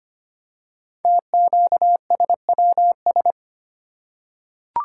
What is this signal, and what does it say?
Morse code, heard as a keyed tone, about 25 wpm, 700 Hz. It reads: TQSWS
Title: Code Groups - 5 Letters - 25wpm